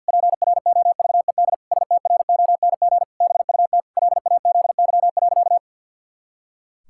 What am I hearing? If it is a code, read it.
Morse code, heard as a keyed tone, about 50 wpm, 700 Hz. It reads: PROVED STRONG BUT LAZY.
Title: Pangram Challenge v2 - 50wpm